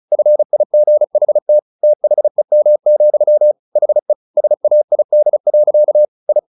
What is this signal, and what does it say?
Morse code, heard as a keyed tone, about 34 wpm, 600 Hz. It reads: FIGHT THEM, HE SAID. I
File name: the old man and the sea 2 35wpm